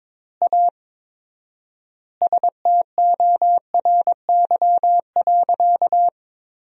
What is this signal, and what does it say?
Morse code, heard as a keyed tone, about 22 wpm, 700 Hz. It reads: A STORY.
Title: 4x Word-Spacing - Sentences from Top 300 Words - 22wpm